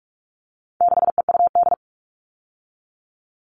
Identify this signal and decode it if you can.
Morse code, heard as a keyed tone, about 44 wpm, 700 Hz. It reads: 6EUD
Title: All Cut Numbers - 4-Digit 45wpm